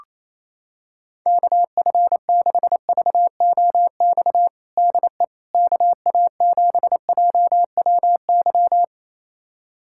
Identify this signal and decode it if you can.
Morse code, heard as a keyed tone, about 28 wpm, 700 Hz. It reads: KF6VOX DE KA7JWY
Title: USA Callsign DE Callsign - 6 Characters - 28wpm